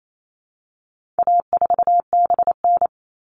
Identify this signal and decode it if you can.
Morse code, heard as a keyed tone, about 28 wpm, 700 Hz. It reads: A4BN
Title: All Cut Numbers - 4-Digit 28wpm